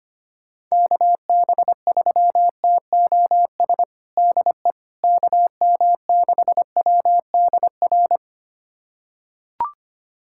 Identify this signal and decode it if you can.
Morse code, heard as a keyed tone, about 25 wpm, 700 Hz. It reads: KB3TOS DE KM6WDR
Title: USA Callsign DE Callsign - 6 Characters - 25wpm